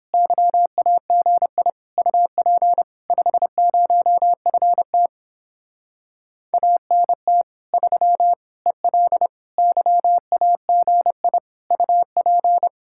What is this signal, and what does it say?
Morse code, heard as a keyed tone, about 29 wpm, 700 Hz. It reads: YAGI UP 50FT ANT 3 EL YAGI UP